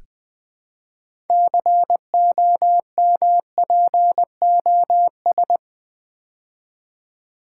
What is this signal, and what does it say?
Morse code, heard as a keyed tone, about 20 wpm, 700 Hz. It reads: COMPOS
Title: Build A Word - 3 Letters At A Time - 8-Letter Words - 20wpm